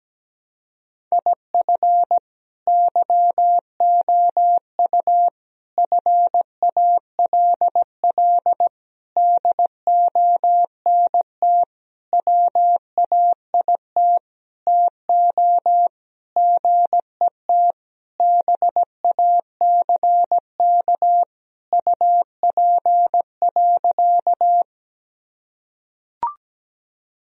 Morse code, heard as a keyed tone, about 17 wpm, 700 Hz. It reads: IF YOU FALL DONT WAIT TO GET BACK UP.
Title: Sentences from Top 1000 Words - Review Part 2 - 17wpm